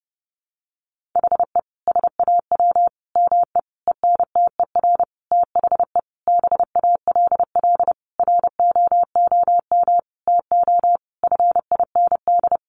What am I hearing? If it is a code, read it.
Morse code, heard as a keyed tone, about 30 wpm, 700 Hz. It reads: HE SAW ME ENTER THE BALL ROOM TO FIND
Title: Sentences from Top 1300 Words 30wpm